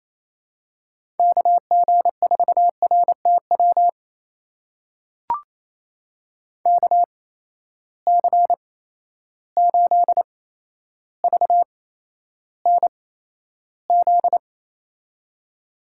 Morse code, heard as a keyed tone, about 28 wpm, spaced as for 12 wpm, 700 Hz. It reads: KG4RTW KC8VNZ
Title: USA Call Signs - 6 Characters - 8x Character Spacing - 28wpm